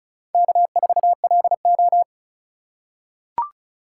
Morse code, heard as a keyed tone, about 34 wpm, 700 Hz. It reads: K4LO
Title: USA Call Signs - 3x Character Spacing - 35wpm